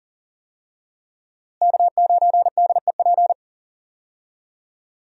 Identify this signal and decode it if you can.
Morse code, heard as a keyed tone, about 40 wpm, 700 Hz. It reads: K9DEP
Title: USA Call Signs - 5 Characters - Repeated 3x - 40wpm